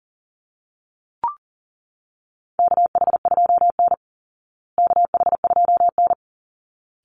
Morse code, heard as a keyed tone, about 38 wpm, 700 Hz. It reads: KH2N KH2N
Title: Call Signs Repeated 3x - 40wpm